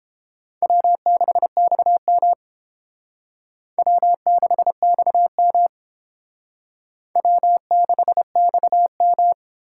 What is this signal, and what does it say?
Morse code, heard as a keyed tone, about 28 wpm, 700 Hz. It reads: W6XM W6XM W6XM